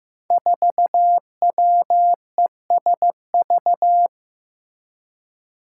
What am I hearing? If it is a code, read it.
Morse code, heard as a keyed tone, about 15 wpm, 700 Hz. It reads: 4WESV
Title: Five Challenging Letter-Number Combinations 15wpm